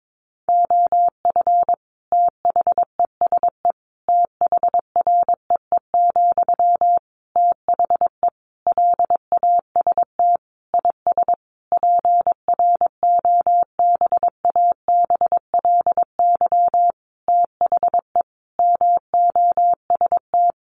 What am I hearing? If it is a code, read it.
Morse code, heard as a keyed tone, about 22 wpm, 700 Hz. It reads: OF THESE THREE, THE LAST IS PROBABLY THE MOST
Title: Sentences from Top 700 Words 22wpm